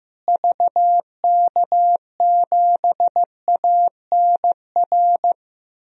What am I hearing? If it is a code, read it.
Morse code, heard as a keyed tone, about 15 wpm, 700 Hz. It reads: VK7ANR
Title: DXCC-100 Call Signs - 6 Characters - Repeated 2x - 15wpm